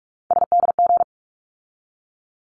Morse code, heard as a keyed tone, about 46 wpm, 700 Hz. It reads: SDG
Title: ARRL Sections 45wpm